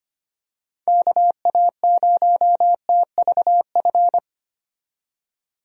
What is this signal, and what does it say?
Morse code, heard as a keyed tone, about 25 wpm, 700 Hz. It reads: KA0TVF
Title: USA Call Signs - 6 Characters - 25wpm